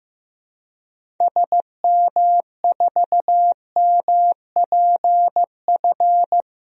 Morse code, heard as a keyed tone, about 15 wpm, 700 Hz. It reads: SM4MPF
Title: DXCC-100 Call Signs - 6 Characters - Repeated 2x - 15wpm